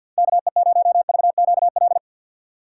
Morse code, heard as a keyed tone, about 47 wpm, 700 Hz. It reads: KE0VQL
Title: USA Call Signs - 6x Character Spacing - 50wpm